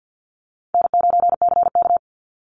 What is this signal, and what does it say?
Morse code, heard as a keyed tone, about 50 wpm, 700 Hz. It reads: N9CK
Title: Easy Call Signs - 50wpm